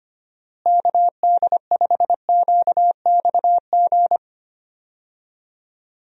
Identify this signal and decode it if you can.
Morse code, heard as a keyed tone, about 25 wpm, 700 Hz. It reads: KD5QXG
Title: USA Call Signs - 6 Characters - Repeated 3x - 25wpm